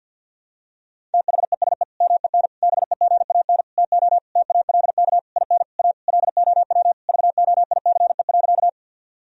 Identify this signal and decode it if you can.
Morse code, heard as a keyed tone, about 51 wpm, 700 Hz. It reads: THESE MEN BEGAN TO TALK IN A LOW VOICE.
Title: Sentences from Top 1000 Words - Review Part 1 - 50wpm